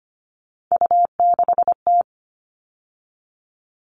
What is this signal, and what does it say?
Morse code, heard as a keyed tone, about 25 wpm, 700 Hz. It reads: U6T
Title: All Cut Numbers - 3-Digit 25wpm